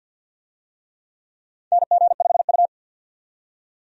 Morse code, heard as a keyed tone, about 53 wpm, 700 Hz. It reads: NG5U
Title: USA Call Signs - 4 Characters - 50wpm